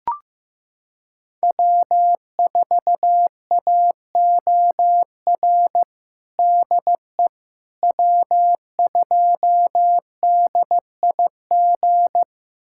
Morse code, heard as a keyed tone, about 15 wpm, 700 Hz. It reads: W4AOR DE W2DIG
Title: USA Callsign DE Callsign - 5 Characters - 15wpm